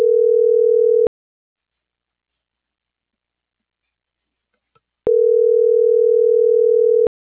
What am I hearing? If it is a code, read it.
Ringback tone (North America), 440 + 480 Hz.